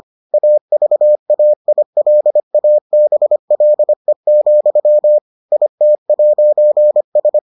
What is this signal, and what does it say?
Morse code, heard as a keyed tone, about 25 wpm, 600 Hz. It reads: AVAILABLE, IT'S